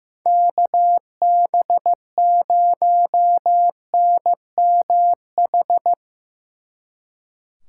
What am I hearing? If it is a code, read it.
Morse code, heard as a keyed tone, about 17 wpm, 700 Hz. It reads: KB0NMH
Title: Speed Racing - USA 2x3 Call Signs 15wpm